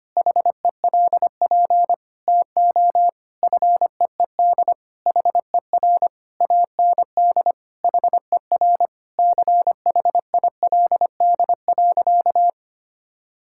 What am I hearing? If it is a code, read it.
Morse code, heard as a keyed tone, about 25 wpm, 700 Hz. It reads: HELP TO FEED HER AND HER CHILD.
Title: Sentences from Top 1600 Words 25wpm